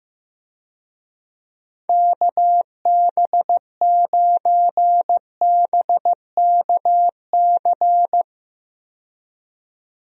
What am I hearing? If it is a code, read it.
Morse code, heard as a keyed tone, about 16 wpm, 700 Hz. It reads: KB9BKC
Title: USA Call Signs - 6 Characters - 15wpm